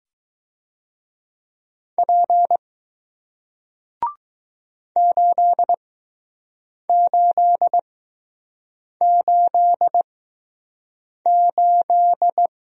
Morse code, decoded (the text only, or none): P 8888